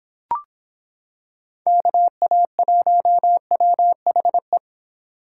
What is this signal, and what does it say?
Morse code, heard as a keyed tone, about 26 wpm, 700 Hz. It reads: KA1WHE